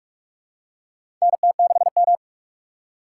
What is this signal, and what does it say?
Morse code, heard as a keyed tone, about 46 wpm, 700 Hz. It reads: NT6M
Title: USA Call Signs - 4 Characters - 2x Character Spacing - 45wpm